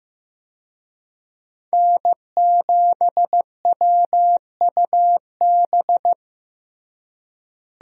Morse code, heard as a keyed tone, about 15 wpm, 700 Hz. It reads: N7WUB
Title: USA Call Signs - 5 Characters - Repeated 2x - 15wpm